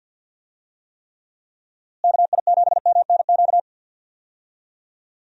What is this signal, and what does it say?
Morse code, heard as a keyed tone, about 46 wpm, 700 Hz. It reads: KI7MNQ
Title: USA Call Signs - 6 Characters - Repeated 2x - 50wpm